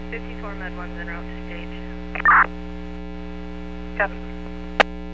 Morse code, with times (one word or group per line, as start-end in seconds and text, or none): none